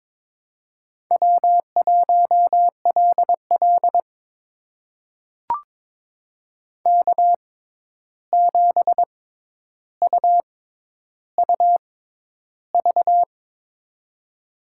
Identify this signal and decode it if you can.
Morse code, heard as a keyed tone, about 23 wpm, spaced as for 11 wpm, 700 Hz. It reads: W1LL K7UUV
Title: USA Call Signs - 6x Character Spacing - 22wpm